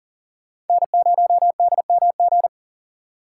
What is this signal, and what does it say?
Morse code, heard as a keyed tone, about 54 wpm, spaced as for 47 wpm, 690 Hz. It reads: N0DMG
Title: Speed Racing - USA Call Signs 40wpm